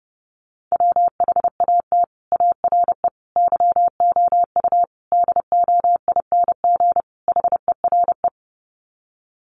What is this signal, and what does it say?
Morse code, heard as a keyed tone, about 28 wpm, 700 Hz. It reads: WHAT ARE YOU DOING HERE
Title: Sets of 5 Words 30wpm